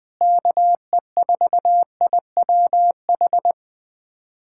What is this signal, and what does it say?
Morse code, heard as a keyed tone, about 20 wpm, 700 Hz. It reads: KE4IWH
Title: USA Callsign DE Callsign - 6 Characters - 20wpm